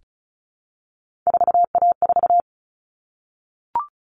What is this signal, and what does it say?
Morse code, heard as a keyed tone, about 34 wpm, 700 Hz. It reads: 4A4
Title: All Cut Numbers - 3-Digit 35wpm